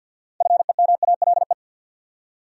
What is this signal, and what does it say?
Morse code, heard as a keyed tone, about 49 wpm, 700 Hz. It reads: FEMALE
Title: Six Letter Words - 50wpm